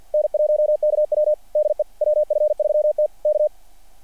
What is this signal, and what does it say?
Morse code, heard as a keyed tone, about 48 wpm, 600 Hz. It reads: N1KW DE WF3T K
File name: archived / CRSnet